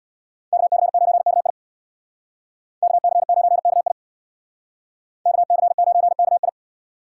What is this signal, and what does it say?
Morse code, heard as a keyed tone, about 65 wpm, 700 Hz. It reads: KC9KI KC9KI KC9KI